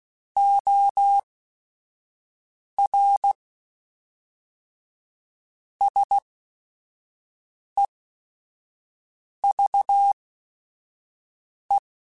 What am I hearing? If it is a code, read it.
Morse code, heard as a keyed tone, about 16 wpm, spaced as for 5 wpm, 800 Hz. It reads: OR SEVE
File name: QSOs / QSOs-mp3s / kc7sac@5